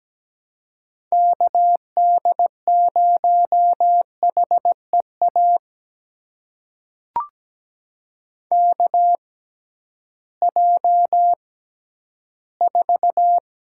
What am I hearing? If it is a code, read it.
Morse code, heard as a keyed tone, about 17 wpm, spaced as for 11 wpm, 700 Hz. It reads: KD0HEA KJ4
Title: USA Call Signs - 6 Characters - 6x Character Spacing - 17wpm